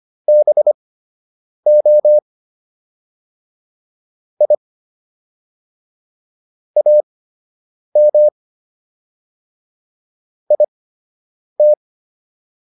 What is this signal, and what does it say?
Morse code, heard as a keyed tone, about 25 wpm, spaced as for 8 wpm, 600 Hz. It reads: BO I AM IT